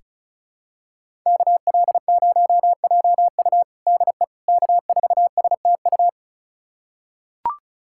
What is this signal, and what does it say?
Morse code, heard as a keyed tone, about 36 wpm, 700 Hz. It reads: KL0JU DE K4STU
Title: USA Callsign DE Callsign - 5 Characters - 35wpm